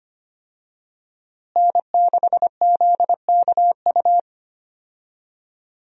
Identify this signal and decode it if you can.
Morse code, heard as a keyed tone, about 25 wpm, 700 Hz. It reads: N6ZKU
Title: USA Call Signs - 5 Characters - Repeated 2x - 25wpm